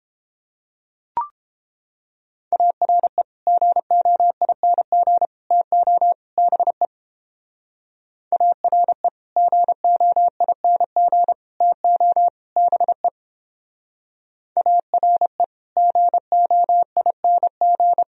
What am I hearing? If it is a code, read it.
Morse code, heard as a keyed tone, about 29 wpm, 700 Hz. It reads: ARE GOING TO BE ARE GOING TO BE ARE GOING